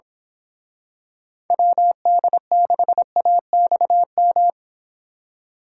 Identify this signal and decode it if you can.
Morse code, heard as a keyed tone, about 26 wpm, 700 Hz. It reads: WD6AXM
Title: Speed Racing - USA 2x3 Call Signs 22wpm